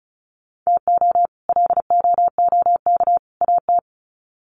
Morse code, heard as a keyed tone, about 34 wpm, 700 Hz. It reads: TO LOOK AT
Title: Sets of 3 Words 35wpm